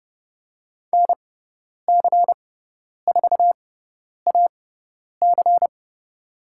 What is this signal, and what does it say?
Morse code, heard as a keyed tone, about 30 wpm, spaced as for 10 wpm, 700 Hz. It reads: NC4AC
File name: Lesson 18g - Letter m - Call Signs 30wpm